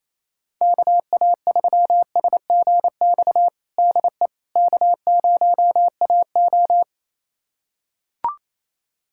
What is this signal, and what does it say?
Morse code, heard as a keyed tone, about 27 wpm, 700 Hz. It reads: KA3SGX DE K0AO
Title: USA Callsign DE Callsign 28wpm